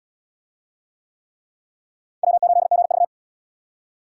Morse code, heard as a keyed tone, about 64 wpm, 700 Hz. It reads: W8GV